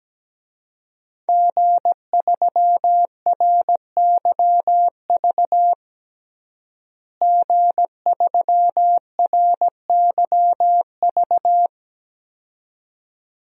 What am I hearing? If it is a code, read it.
Morse code, heard as a keyed tone, about 17 wpm, 700 Hz. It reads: G3RYV G3RYV
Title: DXCC-100 Call Signs - 5 Characters - Repeated 2x - 17wpm